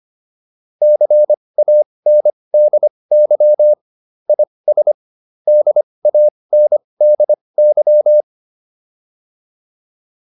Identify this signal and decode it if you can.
Morse code, heard as a keyed tone, about 25 wpm, spaced as for 20 wpm, 600 Hz. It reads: CANDY IS DANDY